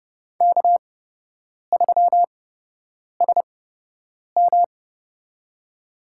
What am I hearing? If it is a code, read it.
Morse code, heard as a keyed tone, about 30 wpm, spaced as for 8 wpm, 700 Hz. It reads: K3SM
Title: USA Call Signs - 8x Character Spacing - 30wpm